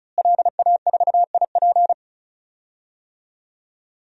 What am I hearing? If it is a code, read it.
Morse code, heard as a keyed tone, about 36 wpm, 700 Hz. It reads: LA4IP
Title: DXCC-100 Call Signs - 5 Characters - Repeated 2x - 35wpm